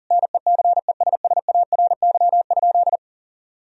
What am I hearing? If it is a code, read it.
Morse code, heard as a keyed tone, about 40 wpm, 700 Hz. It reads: NECESSARY?
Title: Sentences from Top 1000 Words - Review Part 3 - 40wpm